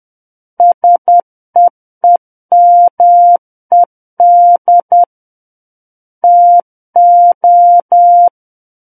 Morse code, heard as a keyed tone, about 10 wpm, 700 Hz. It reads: SEEMED TO
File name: Book One / 01-Chapter 1 - 10 WPM